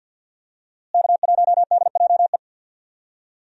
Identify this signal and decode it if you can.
Morse code, heard as a keyed tone, about 46 wpm, 700 Hz. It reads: K1DJE
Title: USA Call Signs - 5 Characters - Repeated 3x - 50wpm